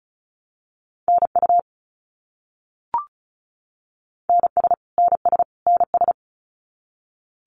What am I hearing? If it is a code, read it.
Morse code, heard as a keyed tone, about 34 wpm, 700 Hz. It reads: NU NS NS NS
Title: Canadian Provinces and Territories - Repeated 3x - 35wpm